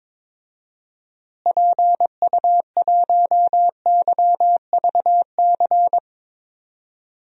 Morse code, heard as a keyed tone, about 21 wpm, 700 Hz. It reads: PU1YVC
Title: DXCC-100 Call Signs - 6 Characters - Repeated 3x - 22wpm